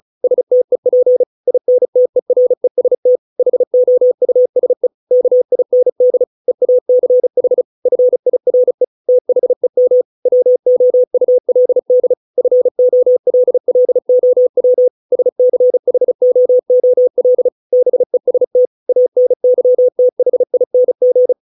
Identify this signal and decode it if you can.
Morse code, heard as a keyed tone, about 34 wpm, 500 Hz. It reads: STEP INTEREST HOUSE KIND EACH FIRE THEM WOULD FOLLOW SCHOOL BEST ANYTHING